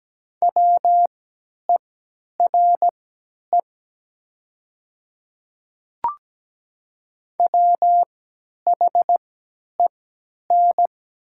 Morse code, heard as a keyed tone, about 17 wpm, spaced as for 10 wpm, 700 Hz. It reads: WERE WHEN
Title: 3x Character Spacing - Top 100 Words - 17wpm